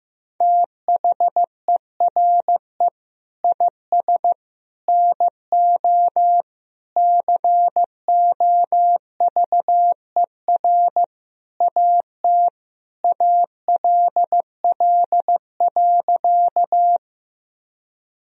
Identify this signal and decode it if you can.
Morse code, heard as a keyed tone, about 15 wpm, 700 Hz. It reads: THERE IS NO COVER AT ALL.
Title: Sentences from Top 1000 Words - Review Part 1 - 15wpm